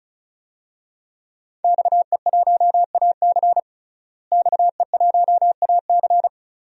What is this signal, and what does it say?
Morse code, heard as a keyed tone, about 34 wpm, 700 Hz. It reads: XE1AC XE1AC
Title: DXCC-100 Call Signs - 5 Characters - Repeated 3x - 35wpm